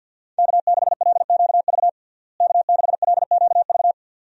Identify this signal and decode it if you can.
Morse code, heard as a keyed tone, about 51 wpm, 700 Hz. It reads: K6LQV K6LQV
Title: USA Call Signs - 5 Characters - Repeated 2x - 50wpm